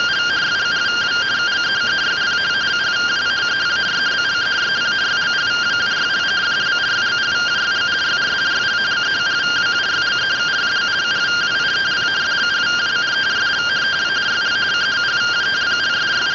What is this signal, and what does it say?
Two-tone teleprinter signal, about 170 Hz shift.